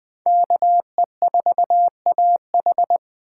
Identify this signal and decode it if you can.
Morse code, heard as a keyed tone, about 21 wpm, 700 Hz. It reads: KE4AH